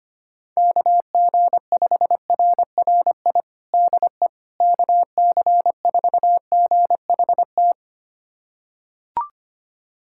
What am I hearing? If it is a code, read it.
Morse code, heard as a keyed tone, about 25 wpm, 700 Hz. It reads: KG5RRI DE KC4GHT
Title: USA Callsign DE Callsign - 6 Characters - 25wpm